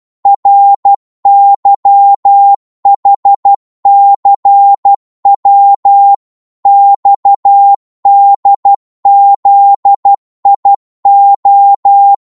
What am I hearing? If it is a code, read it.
Morse code, heard as a keyed tone, about 12 wpm, 800 Hz. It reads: RYHCW XDZIO
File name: Morse / ltr-12wpm